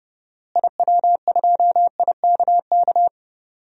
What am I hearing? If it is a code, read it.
Morse code, heard as a keyed tone, about 30 wpm, 700 Hz. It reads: IW2IKK
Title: DXCC-100 Call Signs - 6 Characters - Repeated 2x - 30wpm